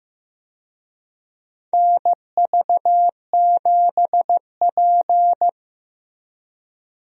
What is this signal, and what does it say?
Morse code, heard as a keyed tone, about 15 wpm, 700 Hz. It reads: NV7P
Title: USA Call Signs - 4 Characters - Repeated 3x - 15wpm